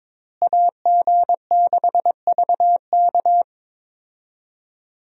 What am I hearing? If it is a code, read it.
Morse code, heard as a keyed tone, about 22 wpm, 700 Hz. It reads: AG6VK
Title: USA Call Signs - 5 Characters - Repeated 3x - 22wpm